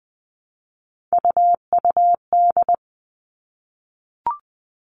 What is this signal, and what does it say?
Morse code, heard as a keyed tone, about 20 wpm, 700 Hz. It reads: UUD
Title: All Cut Numbers - 3-Digit 20wpm